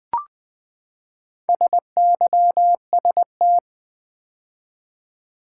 Morse code, heard as a keyed tone, about 20 wpm, 700 Hz. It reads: SYST
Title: Build A Word - 4 Letters At A Time - 10-Letter Words - 20wpm